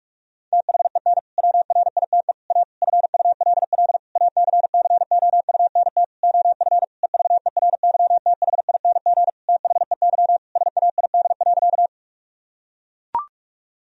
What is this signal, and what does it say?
Morse code, heard as a keyed tone, about 41 wpm, 700 Hz. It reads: THEN WRITE A FULL ACCOUNT OF EVERYTHING THEY SAID.
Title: Sentences from Top 1000 Words - Review Part 3 - 45wpm